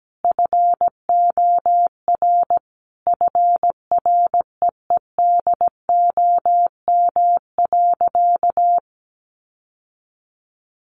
Morse code, heard as a keyed tone, about 17 wpm, 700 Hz. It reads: FOR FREEDOM.
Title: Sentences from Top 1100 Words 17wpm